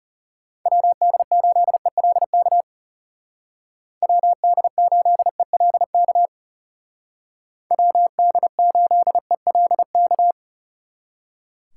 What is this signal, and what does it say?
Morse code, heard as a keyed tone, about 35 wpm, 700 Hz. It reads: WD8ELK WD8ELK WD8ELK
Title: Speed Racing - USA 2x3 Call Signs 30wpm